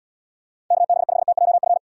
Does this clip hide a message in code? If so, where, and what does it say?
Yes: Morse code from 0.7 to 1.8 s, heard as a keyed tone, about 78 wpm, 700 Hz. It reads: KB5TJF